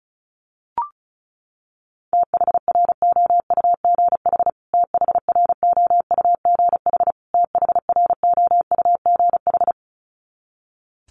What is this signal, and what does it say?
Morse code, heard as a keyed tone, about 34 wpm, 700 Hz. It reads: THROUGH THROUGH THROUGH
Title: Top 500 Words Repeated 35wpm